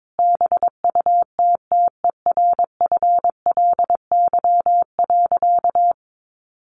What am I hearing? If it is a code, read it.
Morse code, heard as a keyed tone, about 22 wpm, 700 Hz. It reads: BUTTERFLY.
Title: A Sound Of Thunder 22wpm